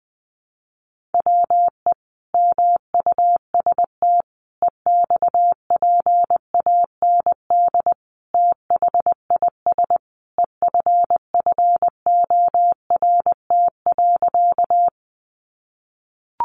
Morse code, heard as a keyed tone, about 20 wpm, 700 Hz. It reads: WE MUST EXPAND THIS EFFORT.
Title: Sentences from Top 1400 Words 20wpm